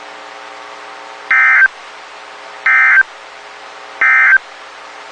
Emergency Alert System (SAME) tones, heard as harsh buzzing bursts. It is heard from 1.2 to 4.5 s.
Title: WXK91 NOAA Weather Radio